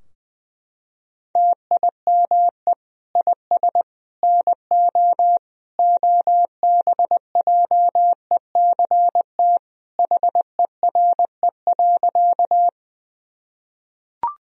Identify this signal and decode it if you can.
Morse code, heard as a keyed tone, about 20 wpm, 700 Hz. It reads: TIME IS NO OBJECT HERE.
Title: Sentences from Top 1000 Words - Review Part 3 - 20wpm